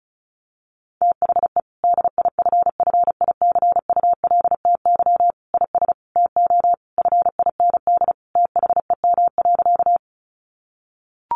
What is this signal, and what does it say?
Morse code, heard as a keyed tone, about 34 wpm, 700 Hz. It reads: THE DIFFICULTY IS TO FIND THEM.
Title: Sentences from Top 1500 Words 35wpm